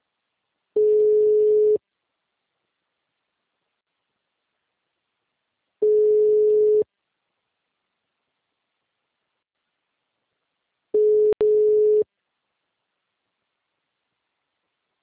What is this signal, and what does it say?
Ringback tone (Europe, 425 Hz).